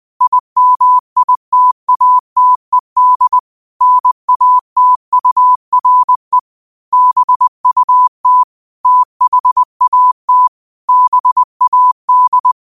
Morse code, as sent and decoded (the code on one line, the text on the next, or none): .. -- .. - .- - . -.. / -. .- - ..- .-. / -... ..- - / - .... .- - / -... .- -..
IMITATED NATURE BUT THAT BAD